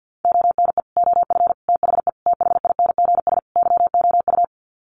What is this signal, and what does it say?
Morse code, heard as a keyed tone, about 48 wpm, 700 Hz. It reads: ONE OF THE THINGS YOU
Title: Sets of 5 Words Encore2 50wpm